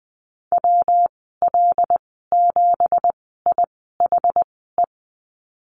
Morse code, heard as a keyed tone, about 19 wpm, spaced as for 14 wpm, 700 Hz. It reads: WL7IHE
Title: 186 Lesson 30g - Letter b - Call Signs 20wpm